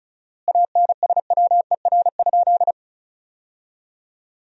Morse code, heard as a keyed tone, about 36 wpm, 700 Hz. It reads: ANSWER?
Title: 4x Word-Spacing - Sentences from Top 300 Words - 35wpm